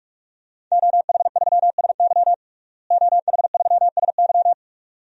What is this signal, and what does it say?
Morse code, heard as a keyed tone, about 47 wpm, 700 Hz. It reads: OH3SY OH3SY
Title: DXCC-100 Call Signs - 5 Characters - Repeated 2x - 45wpm